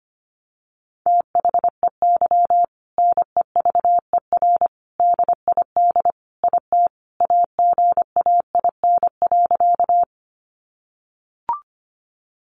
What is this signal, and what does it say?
Morse code, heard as a keyed tone, about 25 wpm, 700 Hz. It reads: THEY NEVER DID IT AGAIN.
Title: Sentences from Top 300 Words 25wpm